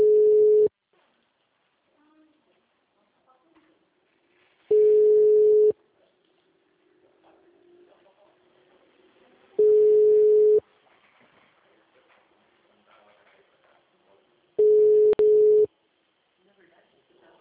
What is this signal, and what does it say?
Ringback tone (Europe, 425 Hz), with speech.
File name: out- -2003- - - .628